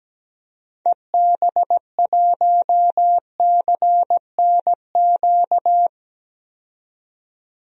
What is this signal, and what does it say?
Morse code, heard as a keyed tone, about 17 wpm, 700 Hz. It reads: EB1CNQ
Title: DXCC-100 Call Signs - 6 Characters - 17wpm